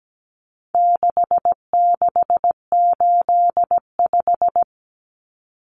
Morse code, heard as a keyed tone, about 17 wpm, 700 Hz. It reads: 6685
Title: Common Cut Numbers - 4-Digit 17wpm